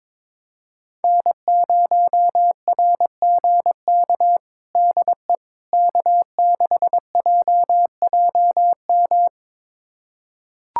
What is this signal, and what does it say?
Morse code, heard as a keyed tone, about 22 wpm, 700 Hz. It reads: N0RGK DE K6JJM
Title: USA Callsign DE Callsign - 5 Characters - 22wpm